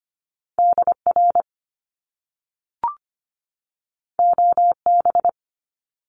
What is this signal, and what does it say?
Morse code, heard as a keyed tone, about 25 wpm, 700 Hz. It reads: DR OB
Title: Top Two Letter Combinations in Words 25wpm